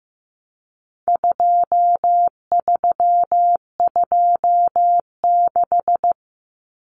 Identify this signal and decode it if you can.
Morse code, heard as a keyed tone, about 15 wpm, 700 Hz. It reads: 2326
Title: Common Cut Numbers - 4-Digit 15wpm